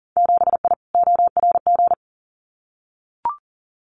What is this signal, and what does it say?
Morse code, heard as a keyed tone, about 40 wpm, 700 Hz. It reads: 7I ORG
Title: Winter Field Day Exchanges Encore 40wpm